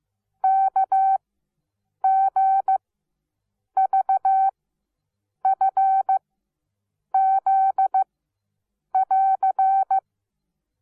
Morse code, heard as a keyed tone, about 15 wpm, spaced as for 7 wpm, 770 Hz. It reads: KGVFZ <AR>
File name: Onlinelehrgang / morsen / DJ4UF Morsen17